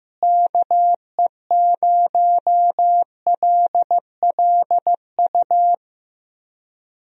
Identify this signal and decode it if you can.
Morse code, heard as a keyed tone, about 15 wpm, 700 Hz. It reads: KE0LLU
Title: USA Call Signs - 6 Characters - Repeated 3x - 15wpm